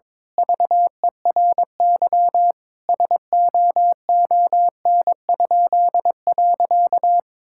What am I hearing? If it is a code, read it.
Morse code, heard as a keyed tone, about 22 wpm, 700 Hz. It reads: VERY SOON?.
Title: Pangram Challenge v2 - 22wpm